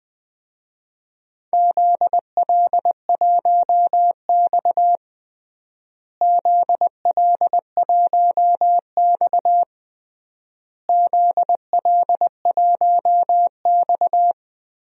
Morse code, heard as a keyed tone, about 19 wpm, 700 Hz. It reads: ZL1X ZL1X ZL1X
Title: DXCC-100 Call Signs - 4 Characters - Repeated 3x - 20wpm